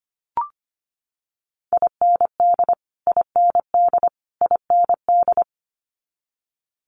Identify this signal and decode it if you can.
Morse code, heard as a keyed tone, about 25 wpm, 700 Hz. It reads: IND IND IND
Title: Multi-letter Phonemes Repeated 3x 25wpm